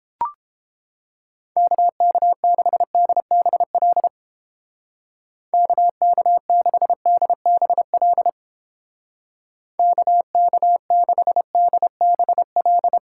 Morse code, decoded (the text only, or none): KK6DBL KK6DBL KK6DBL